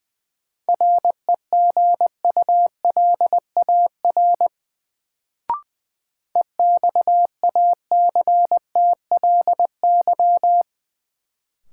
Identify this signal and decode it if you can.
Morse code, heard as a keyed tone, about 20 wpm, 700 Hz. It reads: REGULAR EXACTLY
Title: Sentences from Top 900 Words - New Word Drill - 20wpm